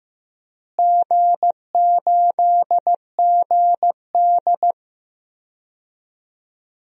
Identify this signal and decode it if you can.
Morse code, heard as a keyed tone, about 15 wpm, 700 Hz. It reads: G8GD
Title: DXCC-100 Call Signs - 4 Characters - Repeated 3x - 15wpm